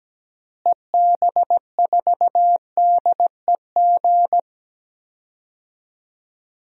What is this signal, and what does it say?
Morse code, heard as a keyed tone, about 17 wpm, 700 Hz. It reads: EB4DEG
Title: DXCC-100 Call Signs - 6 Characters - Repeated 2x - 17wpm